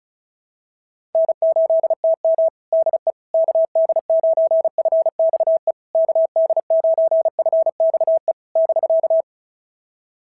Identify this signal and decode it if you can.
Morse code, heard as a keyed tone, about 36 wpm, 690 Hz. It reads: N8TM DE KD9FXE KD9FXE <BK>
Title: SOTA QSO Practice 35wpm